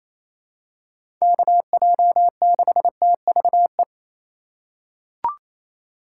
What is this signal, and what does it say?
Morse code, heard as a keyed tone, about 27 wpm, 700 Hz. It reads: KJ6TVE
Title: USA Call Signs - 6 Characters - 3x Character Spacing - 28wpm